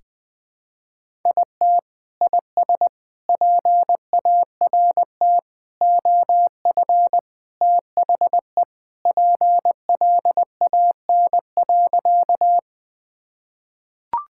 Morse code, heard as a keyed tone, about 20 wpm, 700 Hz. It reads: IT IS PART OF THE PLAN.
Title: Sentences with 4-Word Phrases - 20wpm